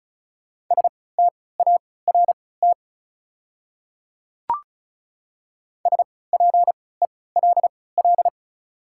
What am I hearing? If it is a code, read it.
Morse code, heard as a keyed tone, about 34 wpm, spaced as for 20 wpm, 690 Hz. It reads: START SPELL
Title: 3x Character Spacing - Top 300 Words - 35wpm